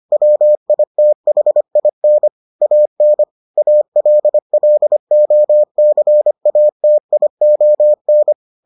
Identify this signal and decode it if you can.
Morse code, heard as a keyed tone, about 25 wpm, 600 Hz. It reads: WITHIN AN ALLOCATION